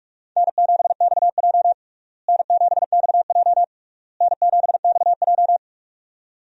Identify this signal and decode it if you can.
Morse code, heard as a keyed tone, about 46 wpm, 700 Hz. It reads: N7XJ N7XJ N7XJ
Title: USA Call Signs - 4 Characters - Repeated 3x - 45wpm